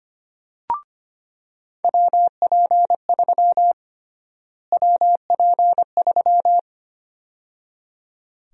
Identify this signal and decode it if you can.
Morse code, heard as a keyed tone, about 25 wpm, 700 Hz. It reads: WP3 WP3
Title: USA Call Sign Prefixes - Repeated 2x - 25wpm